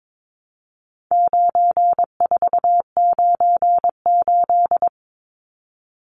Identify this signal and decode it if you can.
Morse code, heard as a keyed tone, about 22 wpm, 700 Hz. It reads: 9498
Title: Numbers - 4-Digit 22wpm